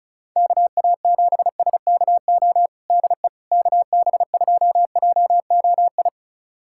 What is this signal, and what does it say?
Morse code, heard as a keyed tone, about 35 wpm, 700 Hz. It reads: KA7SKO DE KB2JOI
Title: USA Callsign DE Callsign - 6 Characters - 35wpm